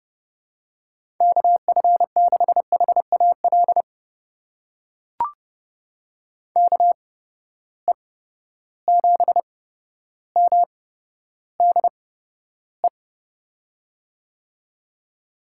Morse code, heard as a keyed tone, about 30 wpm, spaced as for 13 wpm, 700 Hz. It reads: KF6HAL KE7MDE
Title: USA Call Signs - 6 Characters - 8x Character Spacing - 30wpm